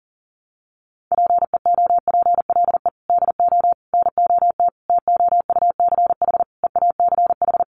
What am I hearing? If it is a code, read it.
Morse code, heard as a keyed tone, about 40 wpm, 700 Hz. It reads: PEOPLE DO NOT TOUCH EACH
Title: Sentences from Top 1200 Words 40wpm